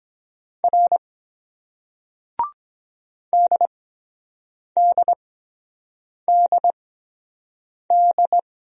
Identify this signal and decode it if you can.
Morse code, heard as a keyed tone, about 23 wpm, spaced as for 7 wpm, 700 Hz. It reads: R DDDD